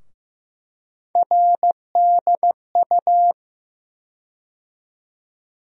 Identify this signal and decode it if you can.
Morse code, heard as a keyed tone, about 15 wpm, 700 Hz. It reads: RDU
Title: USA Call Sign Postfixes - 3 Letters 15wpm